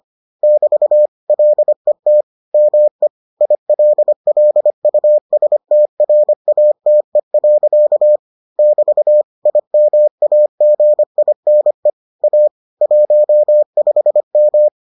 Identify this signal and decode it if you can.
Morse code, heard as a keyed tone, about 25 wpm, 600 Hz. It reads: <BT> LET ME ILLUSTRATE. <BT> IMAGINE A 15M